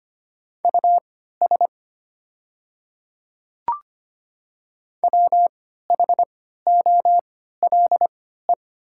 Morse code, heard as a keyed tone, about 25 wpm, spaced as for 14 wpm, 700 Hz. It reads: US WHOLE
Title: 3x Character Spacing - Top 500 Words - 25wpm